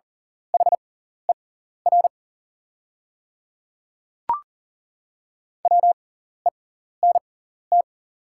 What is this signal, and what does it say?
Morse code, heard as a keyed tone, about 40 wpm, spaced as for 14 wpm, 700 Hz. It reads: HER WENT
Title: 6x Character Spacing - Top 300 Words - 40wpm